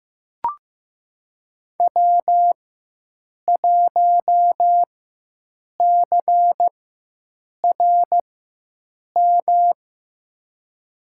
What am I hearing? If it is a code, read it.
Morse code, heard as a keyed tone, about 15 wpm, spaced as for 7 wpm, 700 Hz. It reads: W1CRM
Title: USA Call Signs - 5 Characters - 4x Character Spacing - 15wpm